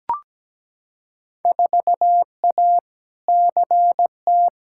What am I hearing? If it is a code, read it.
Morse code, heard as a keyed tone, about 17 wpm, 700 Hz. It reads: 4A CT
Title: Field Day Exchanges Encore - 17wpm